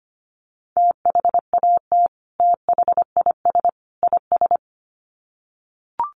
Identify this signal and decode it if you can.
Morse code, heard as a keyed tone, about 25 wpm, 700 Hz. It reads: THAT THIS IS
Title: Sets of 3 Words 25wpm